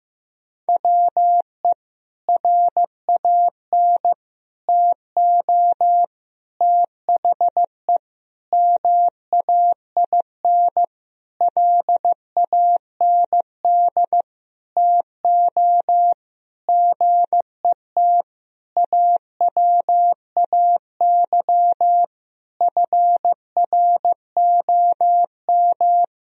Morse code, heard as a keyed tone, about 15 wpm, 700 Hz. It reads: WE RAN TO THE MAIN LAND TO GET AWAY FROM
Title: Sentences from Top 1000 Words - Review Part 2 - 15wpm